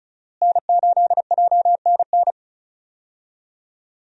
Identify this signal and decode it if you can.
Morse code, heard as a keyed tone, about 36 wpm, 700 Hz. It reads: N8JNN
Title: USA Call Signs - 5 Characters - Repeated 3x - 35wpm